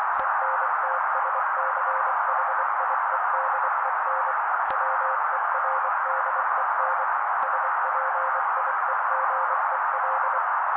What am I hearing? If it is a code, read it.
Morse code, heard as a keyed tone, about 24 wpm, 530 Hz. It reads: ENTSCHIEDEN WERDEN SPIEGEL
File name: daily / 23wpm noise